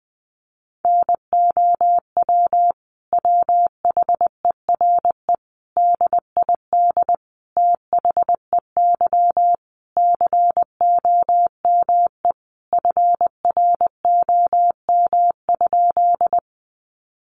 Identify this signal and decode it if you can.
Morse code, heard as a keyed tone, about 20 wpm, 700 Hz. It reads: NOW WHERE DID THEY COME FROM?
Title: Sentences from Top 200 Words 20wpm